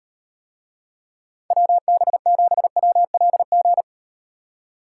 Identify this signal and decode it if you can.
Morse code, heard as a keyed tone, about 38 wpm, 700 Hz. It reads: WB7WLG